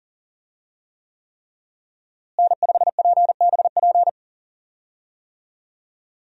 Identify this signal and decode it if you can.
Morse code, heard as a keyed tone, about 38 wpm, 700 Hz. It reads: N5PBP
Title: USA Call Signs - 5 Characters - Repeated 3x - 40wpm